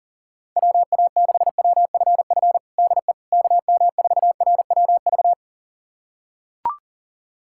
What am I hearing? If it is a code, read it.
Morse code, heard as a keyed tone, about 40 wpm, 700 Hz. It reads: WA6WFF DE KM4RWV
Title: USA Callsign DE Callsign 40wpm